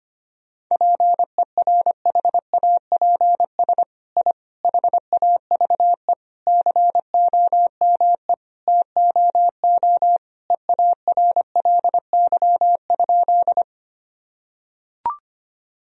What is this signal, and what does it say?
Morse code, heard as a keyed tone, about 25 wpm, 700 Hz. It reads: PERHAPS I HAVE COME TOO EARLY?
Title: Sentences from Top 1000 Words - Review Part 2 - 25wpm